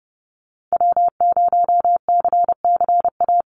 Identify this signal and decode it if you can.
Morse code, heard as a keyed tone, about 30 wpm, 700 Hz. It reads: W0CCA
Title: NA SOTA Activator Callsigns 30wpm